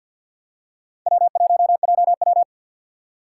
Morse code, heard as a keyed tone, about 52 wpm, 700 Hz. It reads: W1JW
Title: USA Call Signs - 4 Characters - 6x Character Spacing - 50wpm